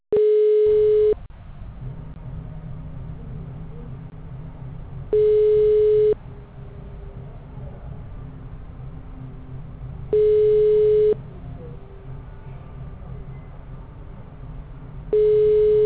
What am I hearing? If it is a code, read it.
Ringback tone (Europe, 425 Hz).
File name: out- -2004- - - .28217